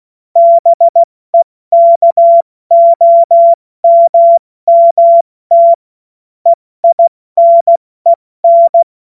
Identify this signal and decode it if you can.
Morse code, heard as a keyed tone, about 16 wpm, 680 Hz. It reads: BEKOMMT EINEN
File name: Morsekurs / Witze / Witz 02 Januar 2024 70 F80